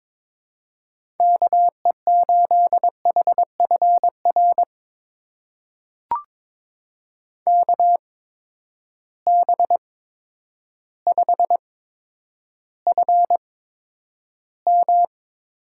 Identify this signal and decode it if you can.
Morse code, heard as a keyed tone, about 22 wpm, spaced as for 10 wpm, 700 Hz. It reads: KE8HFR KB5FM
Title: USA Call Signs - 6 Characters - 8x Character Spacing - 22wpm